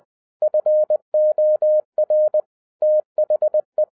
Morse code, heard as a keyed tone, about 20 wpm, 610 Hz. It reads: FOR THE